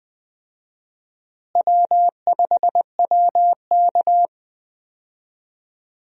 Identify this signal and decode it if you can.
Morse code, heard as a keyed tone, about 20 wpm, 700 Hz. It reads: W5WK
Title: USA Call Signs - 4 Characters - Repeated 2x - 20wpm